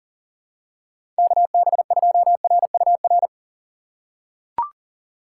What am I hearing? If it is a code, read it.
Morse code, heard as a keyed tone, about 37 wpm, 700 Hz. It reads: KB2RUR
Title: USA Call Signs - 6 Characters - 6x Character Spacing - 40wpm